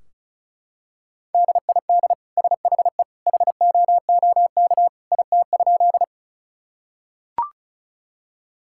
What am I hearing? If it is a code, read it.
Morse code, heard as a keyed tone, about 36 wpm, 700 Hz. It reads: DID SHE HOOK IT?
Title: Sentences from Top 1600 Words 35wpm